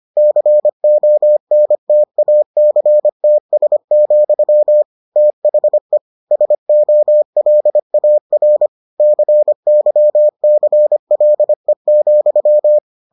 Morse code, heard as a keyed tone, about 25 wpm, 600 Hz. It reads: CONTACTS, THE SOLAR CYCLE,